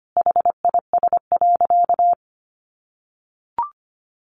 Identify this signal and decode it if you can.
Morse code, heard as a keyed tone, about 25 wpm, 700 Hz. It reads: HIS.
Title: Sentences from Top 1500 Words 25wpm